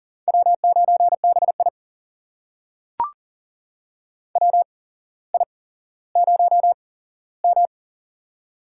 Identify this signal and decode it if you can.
Morse code, heard as a keyed tone, about 42 wpm, spaced as for 17 wpm, 700 Hz. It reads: W9BI WI0M
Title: USA Call Signs - 4 Characters - 8x Character Spacing - 40wpm